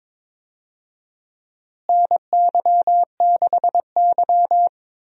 Morse code, heard as a keyed tone, about 22 wpm, 700 Hz. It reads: NY6Y
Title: USA Call Signs - 4 Characters - Repeated 3x - 22wpm